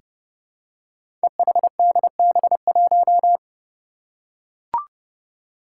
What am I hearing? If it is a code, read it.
Morse code, heard as a keyed tone, about 30 wpm, 700 Hz. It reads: EHDB1
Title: Five Challenging Letter-Number Combinations 30wpm